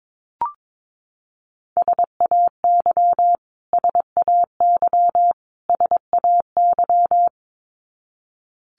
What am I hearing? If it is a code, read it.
Morse code, heard as a keyed tone, about 22 wpm, 700 Hz. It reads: SAY SAY SAY
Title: Top 500 Words Repeated 22wpm